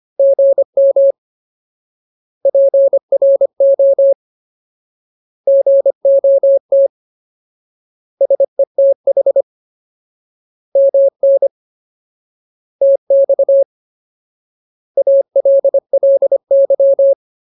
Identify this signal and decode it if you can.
Morse code, heard as a keyed tone, about 25 wpm, 550 Hz. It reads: GM PRO GOT SETH MN TX ALLY
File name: Ultimate Word List / 25WPM Ultimate